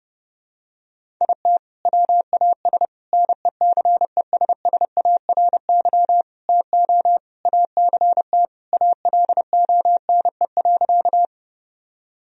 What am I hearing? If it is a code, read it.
Morse code, heard as a keyed tone, about 32 wpm, 700 Hz. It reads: IT WAS NECESSARY TO ACT ALONE.
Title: Sentences from Top 1000 Words - Review Part 1 - 30wpm